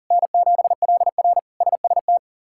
Morse code, heard as a keyed tone, about 40 wpm, 700 Hz. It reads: N7LR SST
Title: K1USN SST QSO Practice - 40wpm